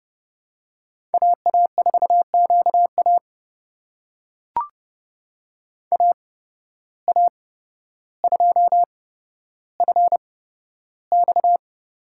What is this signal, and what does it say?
Morse code, heard as a keyed tone, about 29 wpm, spaced as for 13 wpm, 700 Hz. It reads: AA4QA AA2FX
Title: USA Call Signs - 5 Characters - 8x Character Spacing - 30wpm